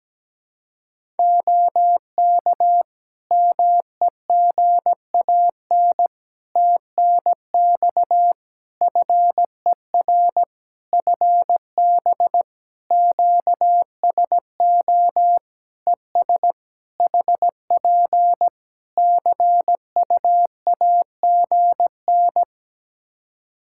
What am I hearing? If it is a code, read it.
Morse code, heard as a keyed tone, about 17 wpm, 700 Hz. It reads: OK MEGAN TNX FER FB QSO ES HP CUAGN
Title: Practice QSO - Exchange 4 - Ending - 17wpm